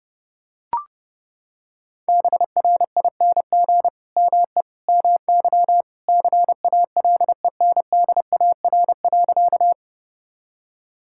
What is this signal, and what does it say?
Morse code, heard as a keyed tone, about 32 wpm, 700 Hz. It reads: BRING ME MY CALENDAR.
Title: Sentences from Top 1600 Words 30wpm